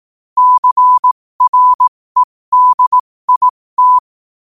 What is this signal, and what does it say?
Morse code, heard as a keyed tone, about 18 wpm, spaced as for 15 wpm, 1000 Hz. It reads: CREDIT
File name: qotdfiles / 15wpm